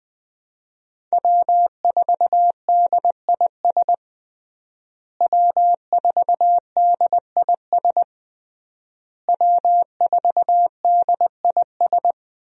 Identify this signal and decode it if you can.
Morse code, heard as a keyed tone, about 21 wpm, 700 Hz. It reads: W4DIS W4DIS W4DIS
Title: USA Call Signs - 5 Characters - Repeated 3x - 20wpm